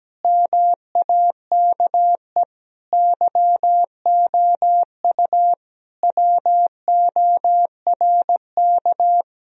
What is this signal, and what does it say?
Morse code, heard as a keyed tone, about 17 wpm, 700 Hz. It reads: MAKE YOU WORK
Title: Sentences from Top 1000 Words - Review Part 1 - 17wpm